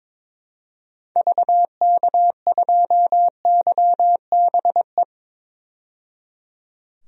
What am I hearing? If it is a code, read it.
Morse code, heard as a keyed tone, about 22 wpm, 700 Hz. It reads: VK2YBE
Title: DXCC-100 Call Signs - 6 Characters - 22wpm